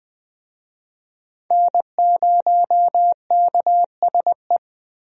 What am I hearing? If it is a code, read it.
Morse code, heard as a keyed tone, about 20 wpm, 700 Hz. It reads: N0KSE
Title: USA Call Signs - 5 Characters - Repeated 2x - 20wpm